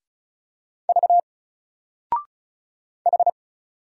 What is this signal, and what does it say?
Morse code, heard as a keyed tone, about 35 wpm, 700 Hz. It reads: V H
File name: Single-Letter-Number-35wpm